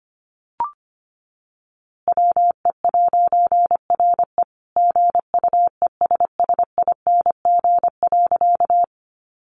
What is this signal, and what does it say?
Morse code, heard as a keyed tone, about 25 wpm, 700 Hz. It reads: WE'RE GUESSING.
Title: A Sound Of Thunder 25wpm